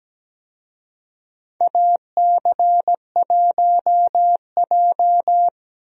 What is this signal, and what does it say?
Morse code, heard as a keyed tone, about 17 wpm, 700 Hz. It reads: AC1J
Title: USA Call Signs - 4 Characters - Repeated 3x - 17wpm